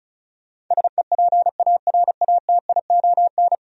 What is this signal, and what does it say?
Morse code, heard as a keyed tone, about 46 wpm, spaced as for 41 wpm, 700 Hz. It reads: SEPARATION
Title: Speed-Racing - Sets of 5 Words 35wpm